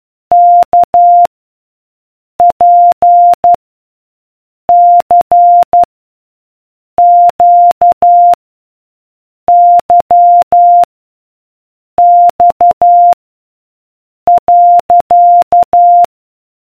Morse code, heard as a keyed tone, about 12 wpm, 700 Hz. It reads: K P C Q Y X .